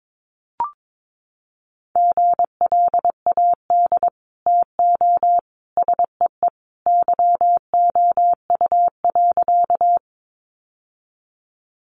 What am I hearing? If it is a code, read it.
Morse code, heard as a keyed tone, about 22 wpm, 700 Hz. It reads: GLAD TO SEE YOU.
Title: Sentences from Top 800 Words 22wpm